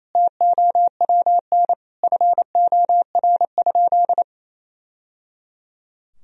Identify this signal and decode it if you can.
Morse code, heard as a keyed tone, about 28 wpm, 700 Hz. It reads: TOWN FOR?
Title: Sentences from Top 1000 Words - Review Part 2 - 28wpm